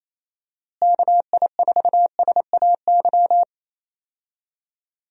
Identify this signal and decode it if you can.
Morse code, heard as a keyed tone, about 27 wpm, 700 Hz. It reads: KI4SAY
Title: USA Call Signs - 6 Characters - Repeated 2x - 28wpm